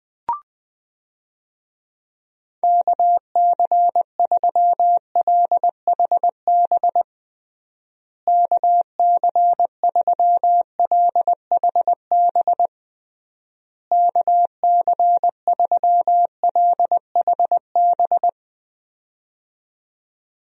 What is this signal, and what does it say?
Morse code, heard as a keyed tone, about 20 wpm, 700 Hz. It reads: KC3LHB KC3LHB KC3LHB
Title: USA Call Signs - 6 Characters - Repeated 3x - 20wpm